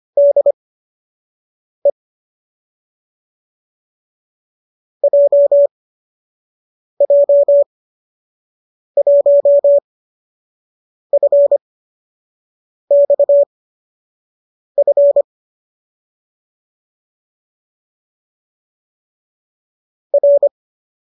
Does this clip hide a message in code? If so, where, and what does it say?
Yes: Morse code from 0.2 to 20.5 s, heard as a keyed tone, about 25 wpm, spaced as for 6 wpm, 570 Hz. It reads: DE JJ1FXF R